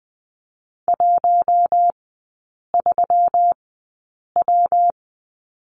Morse code, heard as a keyed tone, about 20 wpm, spaced as for 8 wpm, 700 Hz. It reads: 13W
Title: 172 Lesson 28e - Number 9 - Three Characters 20wpm